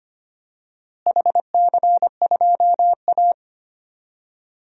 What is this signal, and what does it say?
Morse code, heard as a keyed tone, about 25 wpm, 700 Hz. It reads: HC2A
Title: DXCC-100 Call Signs - 4 Characters - Repeated 3x - 25wpm